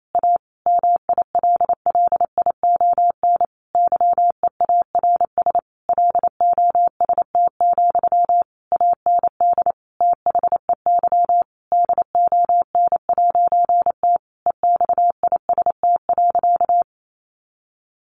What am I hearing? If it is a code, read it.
Morse code, heard as a keyed tone, about 28 wpm, 700 Hz. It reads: A MILLION YEARS LOST, AND THEY DON'T EXIST.
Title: A Sound Of Thunder 28wpm